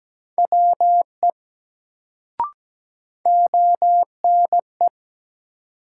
Morse code, heard as a keyed tone, about 17 wpm, 700 Hz. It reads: WE ONE
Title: Sentences from Top 100 Words - New Word Drill - 17wpm